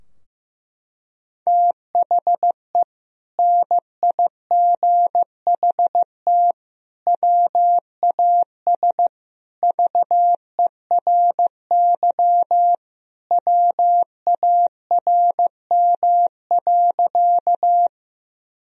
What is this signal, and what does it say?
Morse code, heard as a keyed tone, about 15 wpm, 700 Hz. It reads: THE NIGHT WAS VERY WARM.
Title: Sentences from Top 1000 Words - Review Part 2 - 15wpm